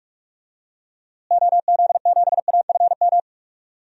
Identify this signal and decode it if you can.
Morse code, heard as a keyed tone, about 46 wpm, 700 Hz. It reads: OZ7AFM
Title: DXCC-100 Call Signs - 6 Characters - Repeated 3x - 45wpm